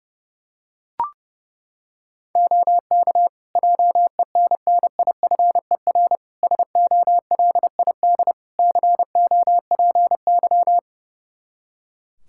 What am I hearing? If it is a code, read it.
Morse code, heard as a keyed tone, about 29 wpm, 700 Hz. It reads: OK JENNIFER SOLID COPY
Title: Practice QSO - Exchange 3 - More Info - 30wpm